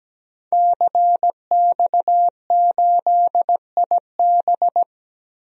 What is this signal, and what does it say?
Morse code, heard as a keyed tone, about 17 wpm, 700 Hz. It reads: CX8IB
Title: DXCC-100 Call Signs - 5 Characters - 17wpm